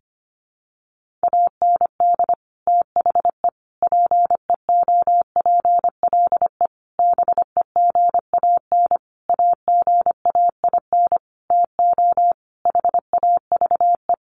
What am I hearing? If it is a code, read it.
Morse code, heard as a keyed tone, about 25 wpm, 700 Hz. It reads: AND THE PEOPLE BEGAN AGAIN TO HAVE
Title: Sentences from Top 1500 Words 25wpm